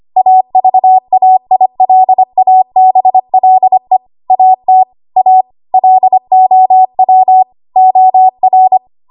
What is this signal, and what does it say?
Morse code, heard as a keyed tone, about 25 wpm, 750 Hz. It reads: AVAILABLE AT A LOW OR
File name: Archive / 25 WPM / 25WPM